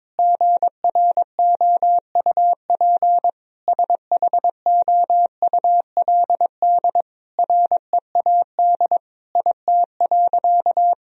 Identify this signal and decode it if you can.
Morse code, heard as a keyed tone, about 21 wpm, 700 Hz. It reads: GROUP SHOULD READ IT.
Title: Sentences from Top 1000 Words - Review Part 1 - 22wpm